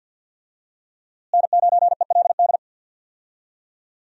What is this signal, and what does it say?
Morse code, heard as a keyed tone, about 52 wpm, 700 Hz. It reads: N9ELD
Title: USA Call Signs - 5 Characters - Repeated 2x - 50wpm